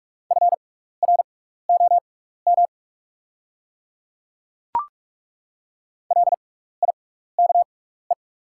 Morse code, heard as a keyed tone, about 48 wpm, spaced as for 16 wpm, 700 Hz. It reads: FROM LIKE
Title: 6x Character Spacing - Top 100 Words - 45wpm